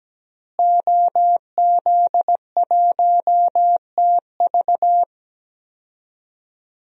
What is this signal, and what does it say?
Morse code, heard as a keyed tone, about 17 wpm, 700 Hz. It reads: OZ1TV
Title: DXCC-100 Call Signs - 5 Characters - Repeated 3x - 17wpm